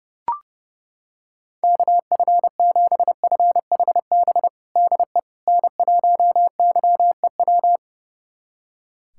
Morse code, heard as a keyed tone, about 32 wpm, 700 Hz. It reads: KF7FHB DE N1YEW
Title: USA Callsign DE Callsign 30wpm